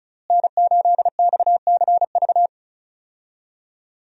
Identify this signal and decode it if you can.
Morse code, heard as a keyed tone, about 34 wpm, 700 Hz. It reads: N8XCV
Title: USA Call Signs - 5 Characters - Repeated 3x - 35wpm